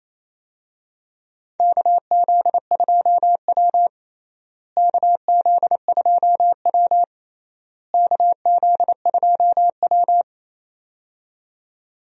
Morse code, heard as a keyed tone, about 28 wpm, 700 Hz. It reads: KZ2W KZ2W KZ2W
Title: USA Call Signs - 4 Characters - Repeated 3x - 28wpm